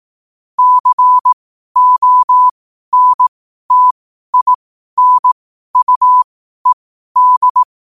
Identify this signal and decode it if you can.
Morse code, heard as a keyed tone, about 18 wpm, spaced as for 12 wpm, 1000 Hz. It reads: CONTINUED
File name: qotdfiles / 12wpm